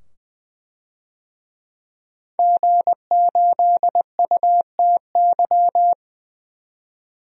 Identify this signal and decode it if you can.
Morse code, heard as a keyed tone, about 20 wpm, 700 Hz. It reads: G8UTY
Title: DXCC-100 Call Signs - 5 Characters - 20wpm